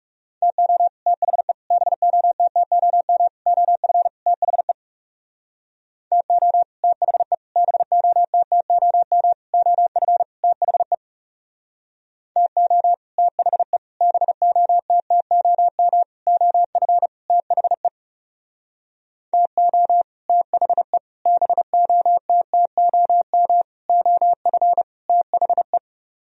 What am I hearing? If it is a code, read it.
Morse code, heard as a keyed tone, about 39 wpm, 700 Hz. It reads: TO THE BOTTOM OF THE TO THE BOTTOM OF THE TO THE BOTTOM OF THE TO THE BOTTOM OF THE